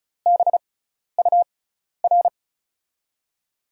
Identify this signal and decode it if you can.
Morse code, heard as a keyed tone, about 34 wpm, spaced as for 12 wpm, 700 Hz. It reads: BUR
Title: USA Call Signs - 5 Characters - 6x Character Spacing - 35wpm